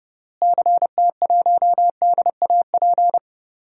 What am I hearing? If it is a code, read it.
Morse code, heard as a keyed tone, about 30 wpm, 700 Hz. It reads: CT1DAP
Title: DXCC-100 Call Signs - 6 Characters - Repeated 3x - 30wpm